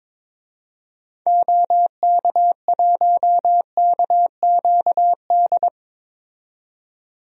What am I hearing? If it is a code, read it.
Morse code, heard as a keyed tone, about 22 wpm, 700 Hz. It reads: OK1KQD
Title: DXCC-100 Call Signs - 6 Characters - 22wpm